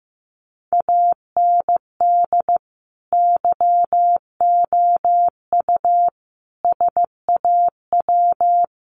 Morse code, heard as a keyed tone, about 14 wpm, 700 Hz. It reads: AND YOU SAW
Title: Sentences from Top 1300 Words 15wpm